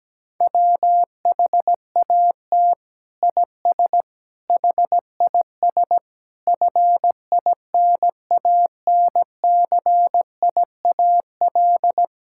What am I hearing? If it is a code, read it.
Morse code, heard as a keyed tone, about 17 wpm, 700 Hz. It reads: WHAT IS HIS FINANCIAL
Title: Sentences from Top 1000 Words - Review Part 5 - 17wpm